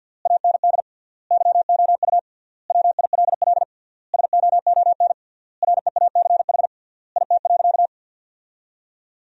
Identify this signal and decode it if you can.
Morse code, heard as a keyed tone, about 51 wpm, 700 Hz. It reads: AND YOU WILL SOON REACH IT.
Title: 3x Word-Spacing - Sentences from Top 400 Words - 50wpm